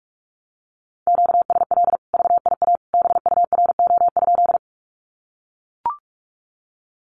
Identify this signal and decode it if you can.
Morse code, heard as a keyed tone, about 44 wpm, 700 Hz. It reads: QSL VIA BURO?
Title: Common QSO Phrases 45wpm